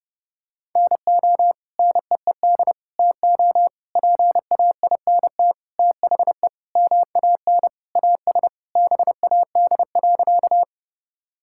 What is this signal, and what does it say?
Morse code, heard as a keyed tone, about 32 wpm, 700 Hz. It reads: NO NEED TO PAINT THE MAN AS BAD.
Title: Sentences from Top 1000 Words - Review Part 6 - 30wpm